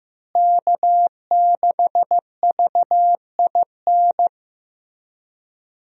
Morse code, heard as a keyed tone, about 15 wpm, 700 Hz. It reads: K6VIN
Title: USA Call Signs - 5 Characters - Repeated 3x - 15wpm